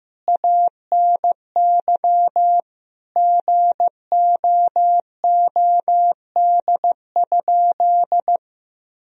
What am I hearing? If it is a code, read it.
Morse code, heard as a keyed tone, about 15 wpm, 700 Hz. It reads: ANY GOOD?
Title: Sentences from Top 1000 Words - Review Part 4 - 15wpm